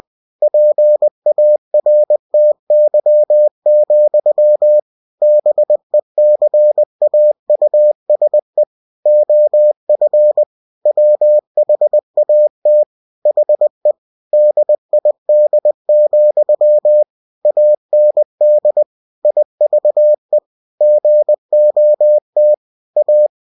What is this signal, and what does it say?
Morse code, heard as a keyed tone, about 20 wpm, 600 Hz. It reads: PARTY, BECAUSE OF WHAT HE DID, AND IVE GOT A
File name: cw-books / winnie-the-pooh / winnie-20wpm-page-0107